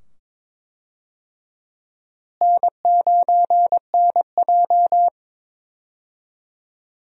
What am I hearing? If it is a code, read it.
Morse code, heard as a keyed tone, about 22 wpm, 700 Hz. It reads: N9NJ
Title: USA Call Signs - 4 Characters - Repeated 2x - 22wpm